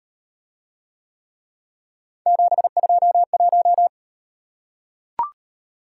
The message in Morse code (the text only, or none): 721